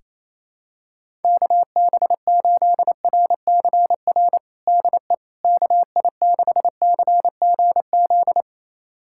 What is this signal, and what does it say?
Morse code, heard as a keyed tone, about 28 wpm, 700 Hz. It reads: KB8RCR DE KI6CGZ
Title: USA Callsign DE Callsign - 6 Characters - 28wpm